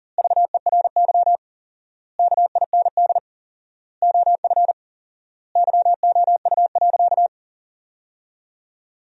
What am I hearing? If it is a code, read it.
Morse code, heard as a keyed tone, about 41 wpm, 700 Hz. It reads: VERY KIND OF YOU.
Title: 4x Word-Spacing - Sentences from Top 200 Words - 40wpm